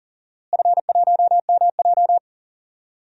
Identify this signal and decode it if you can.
Morse code, heard as a keyed tone, about 38 wpm, 700 Hz. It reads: F1MJ
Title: DXCC-100 Call Signs - 4 Characters - Repeated 2x - 40wpm